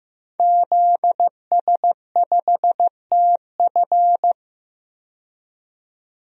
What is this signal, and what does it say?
Morse code, heard as a keyed tone, about 16 wpm, 700 Hz. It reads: ZS5TF
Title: DXCC-100 Call Signs - 5 Characters - 15wpm